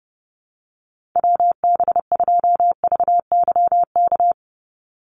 Morse code, heard as a keyed tone, about 30 wpm, 700 Hz. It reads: WB2VYK
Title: Call Signs Repeated 2x - 30wpm